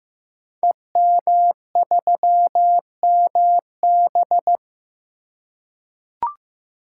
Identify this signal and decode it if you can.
Morse code, heard as a keyed tone, about 15 wpm, 700 Hz. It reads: EM3MB
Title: Code Groups - 5 Letter:Numbers - 15wpm